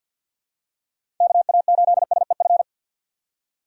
Morse code, heard as a keyed tone, about 52 wpm, 700 Hz. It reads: KA8SEF
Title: USA Call Signs - 6 Characters - 4x Character Spacing - 50wpm